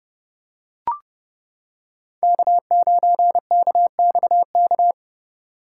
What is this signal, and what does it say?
Morse code, heard as a keyed tone, about 28 wpm, 700 Hz. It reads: K9KXK